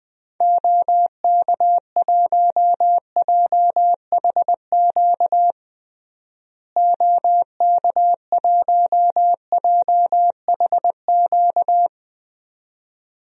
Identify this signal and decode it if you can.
Morse code, heard as a keyed tone, about 20 wpm, 700 Hz. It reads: OK1JHQ OK1JHQ
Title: DXCC-100 Call Signs - 6 Characters - Repeated 2x - 20wpm